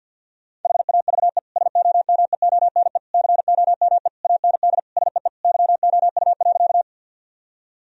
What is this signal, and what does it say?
Morse code, heard as a keyed tone, about 50 wpm, 700 Hz. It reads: HAVE SOMEONE COME AND SEE YOU.
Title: Sentences from Top 1000 Words - Review Part 4 - 50wpm